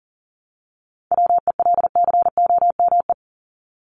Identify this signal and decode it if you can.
Morse code, heard as a keyed tone, about 40 wpm, 700 Hz. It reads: WELCOME
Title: Top 1751-2000 Words 40wpm